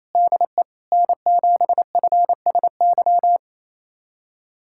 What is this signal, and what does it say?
Morse code, heard as a keyed tone, about 28 wpm, 700 Hz. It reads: DE N7FSY
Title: USA Callsign DE Callsign - 5 Characters - 28wpm